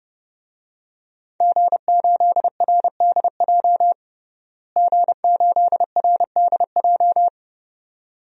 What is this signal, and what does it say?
Morse code, heard as a keyed tone, about 30 wpm, 700 Hz. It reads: G8RDJ G8RDJ
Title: DXCC-100 Call Signs - 5 Characters - Repeated 2x - 30wpm